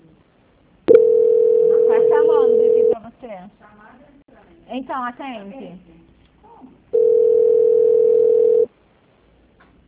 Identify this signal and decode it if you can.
Ringback tone (North America), 440 + 480 Hz, with speech.
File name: out- -2030- - - .146